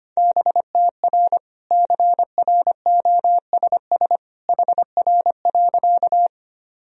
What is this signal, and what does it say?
Morse code, heard as a keyed tone, about 25 wpm, 700 Hz. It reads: BTR CROSS HR.
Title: Sentences from Top 400 Words and Abbreviations 25wpm